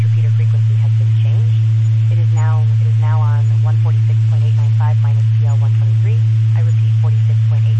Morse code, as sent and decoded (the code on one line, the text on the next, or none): none